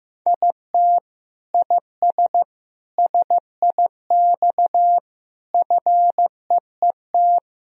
Morse code, heard as a keyed tone, about 15 wpm, 700 Hz. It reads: IT IS SIX FEET
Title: Sentences from Top 1000 Words - Review Part 2 - 15wpm